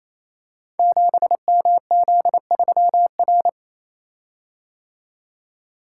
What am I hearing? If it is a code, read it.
Morse code, heard as a keyed tone, about 27 wpm, 700 Hz. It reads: 7MZ3R
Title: Code Groups - 5 Letter:Numbers - 28wpm